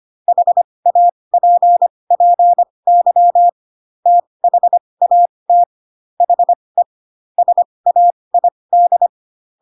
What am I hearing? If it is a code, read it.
Morse code, heard as a keyed tone, about 25 wpm, spaced as for 20 wpm, 700 Hz. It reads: HAPPY THAT HE SAID
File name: Pinocchio / Pinocchio ch8 15 25 ik1hge